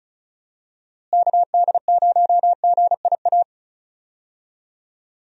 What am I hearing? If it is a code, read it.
Morse code, heard as a keyed tone, about 36 wpm, 700 Hz. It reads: KD0GIA
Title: USA Call Signs - 6 Characters - Repeated 3x - 35wpm